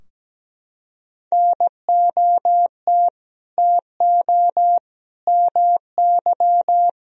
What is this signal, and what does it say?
Morse code, heard as a keyed tone, about 17 wpm, 700 Hz. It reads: NOT TO MY
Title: Sentences from Top 1000 Words - Review Part 4 - 17wpm